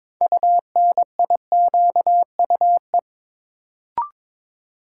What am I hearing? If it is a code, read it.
Morse code, heard as a keyed tone, about 22 wpm, 700 Hz. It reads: UNIQUE
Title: Sentences from Top 1000 Words - Review Part 7 - New Word Drill - 22wpm